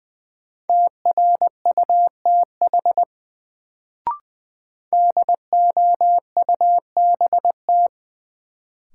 Morse code, heard as a keyed tone, about 20 wpm, 700 Hz. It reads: TRUTH DOUBT
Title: Sets of 5 Words - New Word Drill - 20wpm